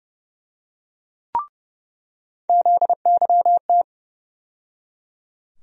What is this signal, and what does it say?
Morse code, heard as a keyed tone, about 30 wpm, 700 Hz. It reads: ZYT
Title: USA Call Sign Postfixes - 3 Letters 30wpm